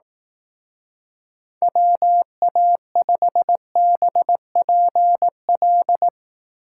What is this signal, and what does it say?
Morse code, heard as a keyed tone, about 19 wpm, 700 Hz. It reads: WA5BPL
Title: Speed Racing - USA Call Signs 15wpm